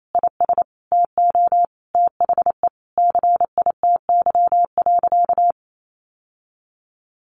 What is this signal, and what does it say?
Morse code, heard as a keyed tone, about 28 wpm, 700 Hz. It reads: IS TO THE CITY.
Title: Sentences from Top 1200 Words 28wpm